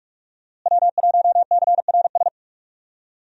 Morse code, heard as a keyed tone, about 41 wpm, 700 Hz. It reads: W1CRS
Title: USA Call Signs - 5 Characters - Repeated 3x - 45wpm